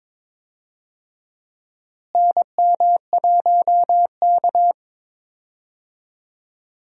Morse code, heard as a keyed tone, about 22 wpm, 700 Hz. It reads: NM1K
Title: USA Call Signs - 4 Characters - Repeated 3x - 22wpm